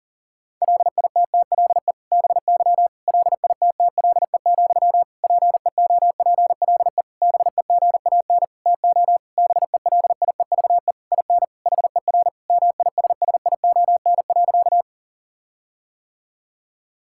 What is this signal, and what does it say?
Morse code, heard as a keyed tone, about 40 wpm, 700 Hz. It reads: LITTLE BY LITTLE, PEOPLE BEGAN TO BELIEVE IN HER MISSION.
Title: Sentences from Top 1600 Words 40wpm